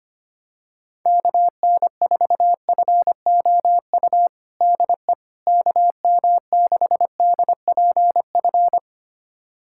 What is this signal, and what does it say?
Morse code, heard as a keyed tone, about 25 wpm, 700 Hz. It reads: KN4FOU DE KM6DPF
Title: USA Callsign DE Callsign - 6 Characters - 25wpm